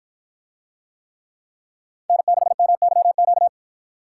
Morse code, heard as a keyed tone, about 52 wpm, 700 Hz. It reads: N6MYQ